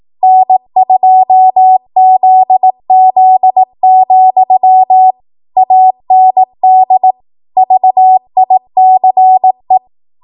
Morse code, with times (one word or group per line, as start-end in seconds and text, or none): 0.2-5.1 N2ZZ,
5.6-7.1 AND
7.6-9.8 VICE